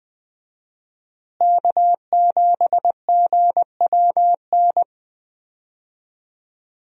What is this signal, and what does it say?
Morse code, heard as a keyed tone, about 20 wpm, 700 Hz. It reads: K7GWN
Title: USA Call Signs - 5 Characters - Repeated 3x - 20wpm